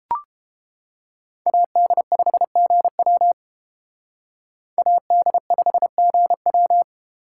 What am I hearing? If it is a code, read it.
Morse code, heard as a keyed tone, about 29 wpm, 700 Hz. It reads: AD5GW AD5GW